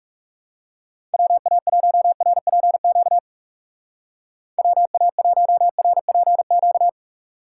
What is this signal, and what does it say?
Morse code, heard as a keyed tone, about 36 wpm, 700 Hz. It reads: WA1RPQ WA1RPQ
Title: Speed Racing - USA 2x3 Call Signs 30wpm